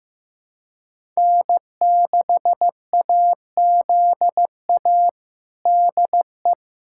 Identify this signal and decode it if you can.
Morse code, heard as a keyed tone, about 15 wpm, 700 Hz. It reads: N6AZA DE
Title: USA Callsign DE Callsign - 5 Characters - 15wpm